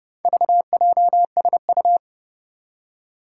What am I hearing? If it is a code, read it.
Morse code, heard as a keyed tone, about 30 wpm, 700 Hz. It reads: VJSU
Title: Four Challenging Letter-Number Combinations 30wpm